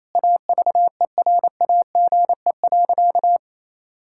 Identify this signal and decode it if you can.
Morse code, heard as a keyed tone, about 28 wpm, 700 Hz. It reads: AVERAGE.
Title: Sentences from Top 1000 Words - Review Part 6 - 28wpm